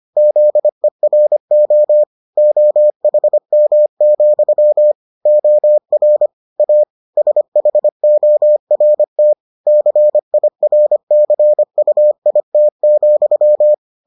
Morse code, heard as a keyed tone, about 25 wpm, 600 Hz. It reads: ZERO OHM, OR A SHORT CIRCUIT,